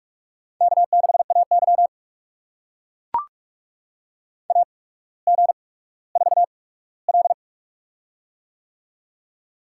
Morse code, heard as a keyed tone, about 44 wpm, spaced as for 19 wpm, 700 Hz. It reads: K6AY AG4L
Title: USA Call Signs - 8x Character Spacing - 45wpm